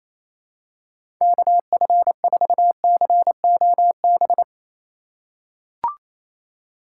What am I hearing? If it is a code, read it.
Morse code, heard as a keyed tone, about 28 wpm, 700 Hz. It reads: KF4COB
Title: USA Call Signs - 6 Characters - 2x Character Spacing - 28wpm